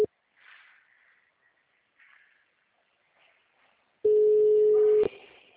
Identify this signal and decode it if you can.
Ringback tone (Europe, 425 Hz).
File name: out- -2002- - - .35596